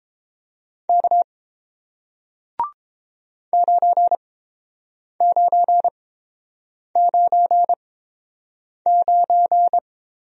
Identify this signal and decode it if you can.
Morse code, heard as a keyed tone, about 31 wpm, spaced as for 7 wpm, 700 Hz. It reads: K 9999